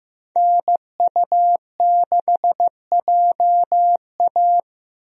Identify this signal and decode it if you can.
Morse code, heard as a keyed tone, about 16 wpm, 700 Hz. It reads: NU6JA
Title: USA Call Signs - 5 Characters - Repeated 2x - 15wpm